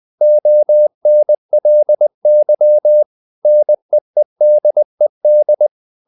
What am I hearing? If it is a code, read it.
Morse code, heard as a keyed tone, about 20 wpm, 600 Hz. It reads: ONLY NEEDED
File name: the old man and the sea part 2 20wpm